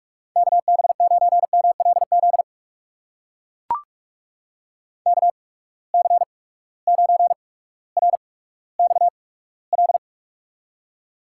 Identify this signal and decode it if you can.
Morse code, heard as a keyed tone, about 44 wpm, spaced as for 19 wpm, 700 Hz. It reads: KB9MLZ KC9RXL
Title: USA Call Signs - 8x Character Spacing - 45wpm